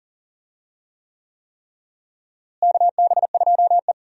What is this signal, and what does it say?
Morse code, heard as a keyed tone, about 40 wpm, 700 Hz. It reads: KB2E
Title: USA Call Signs - 4 Characters - Repeated 3x - 40wpm